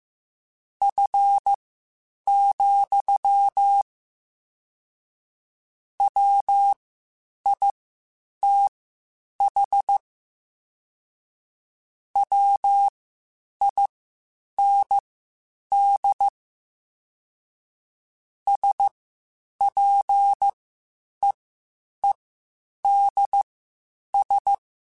Morse code, heard as a keyed tone, about 15 wpm, spaced as for 8 wpm, 800 Hz. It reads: F, WITH WIND SPEEDS
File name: Spaced Characters / Spaced Characters-mp3s / Weather 4 24 sp @13